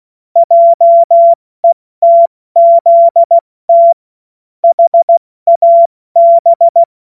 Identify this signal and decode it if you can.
Morse code, heard as a keyed tone, about 16 wpm, 680 Hz. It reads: JETZT HAB
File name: Morsekurs / Witze / Witz 11. Mai 2021, 70 F80